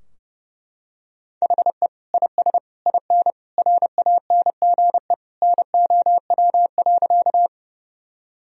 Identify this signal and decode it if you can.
Morse code, heard as a keyed tone, about 30 wpm, 700 Hz. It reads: HE IS IN RANGE NOW.
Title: Sentences from Top 1000 Words - Review Part 4 - 30wpm